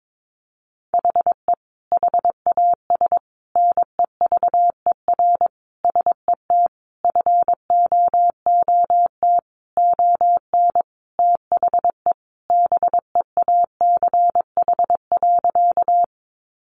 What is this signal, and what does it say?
Morse code, heard as a keyed tone, about 22 wpm, 700 Hz. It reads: HE HAS NEVER SET FOOT ON THE BEACH.
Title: Sentences from Top 1500 Words 22wpm